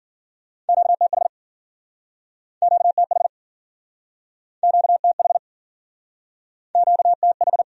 Morse code, heard as a keyed tone, about 54 wpm, 700 Hz. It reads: QTH QTH QTH QTH